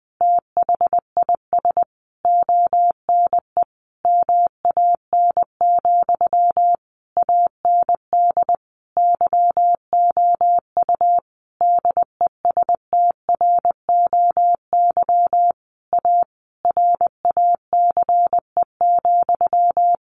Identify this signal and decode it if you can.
Morse code, heard as a keyed tone, about 20 wpm, 700 Hz. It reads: THIS ONE MAN, AND YOU DESTROY A RACE,
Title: A Sound Of Thunder 20wpm